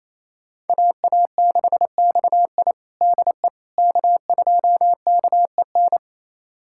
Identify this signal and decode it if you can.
Morse code, heard as a keyed tone, about 27 wpm, 700 Hz. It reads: AA6XI DE K2KEN
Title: USA Callsign DE Callsign - 5 Characters - 28wpm